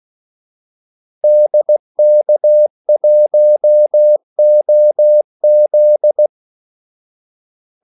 Morse code, heard as a keyed tone, about 16 wpm, 600 Hz. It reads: DK1OZ
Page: Kaldesignaler 16 wpm